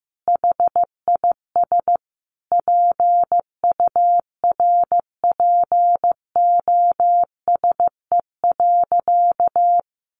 Morse code, heard as a keyed tone, about 15 wpm, 700 Hz. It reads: HIS PURPOSE.
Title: Sentences from Top 1400 Words 15wpm